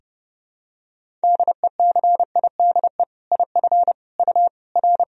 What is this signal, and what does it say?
Morse code, heard as a keyed tone, about 30 wpm, 700 Hz. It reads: DECIDE IF U R
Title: Sentences from Top 500 Words and Abbreviations 30wpm